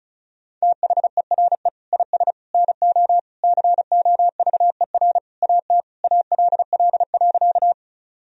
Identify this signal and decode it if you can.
Morse code, heard as a keyed tone, about 34 wpm, 700 Hz. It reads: THERE IS NO COVER AT ALL.
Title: Sentences from Top 1000 Words - Review Part 1 - 35wpm